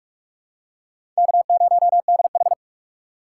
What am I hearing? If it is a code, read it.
Morse code, heard as a keyed tone, about 46 wpm, 700 Hz. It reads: K0DH
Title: USA Call Signs - 4 Characters - 2x Character Spacing - 45wpm